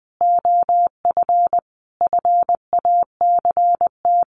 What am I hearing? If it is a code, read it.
Morse code, heard as a keyed tone, about 20 wpm, 700 Hz. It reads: OF FACT
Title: Sets of 5 Words 20wpm